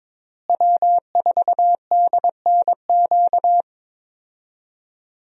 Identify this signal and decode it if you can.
Morse code, heard as a keyed tone, about 21 wpm, 700 Hz. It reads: W4DNQ
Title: USA Call Signs - 5 Characters - Repeated 3x - 22wpm